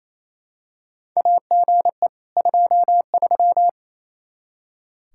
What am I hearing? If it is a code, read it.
Morse code, heard as a keyed tone, about 28 wpm, 700 Hz. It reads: AGE 23
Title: Practice QSO - Exchange 3 - More Info - 28wpm